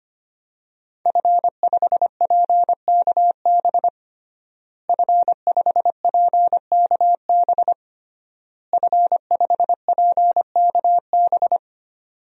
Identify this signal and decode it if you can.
Morse code, heard as a keyed tone, about 25 wpm, 700 Hz. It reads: F5PKB F5PKB F5PKB
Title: DXCC-100 Call Signs - 5 Characters - Repeated 3x - 25wpm